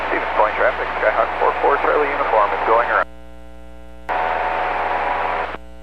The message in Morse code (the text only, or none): none